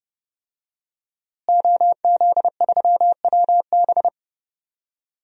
Morse code, heard as a keyed tone, about 30 wpm, 700 Hz. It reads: OZ3WB
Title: DXCC-100 Call Signs - 5 Characters - Repeated 3x - 30wpm